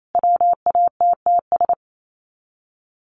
Morse code, heard as a keyed tone, about 28 wpm, 700 Hz. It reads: WATTS
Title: Common QSO Phrases 28wpm